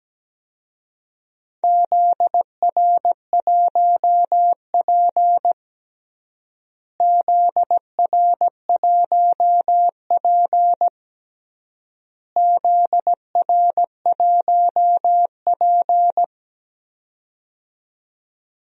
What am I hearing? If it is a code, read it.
Morse code, heard as a keyed tone, about 17 wpm, 700 Hz. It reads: ZR1P ZR1P ZR1P
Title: DXCC-100 Call Signs - 4 Characters - Repeated 3x - 17wpm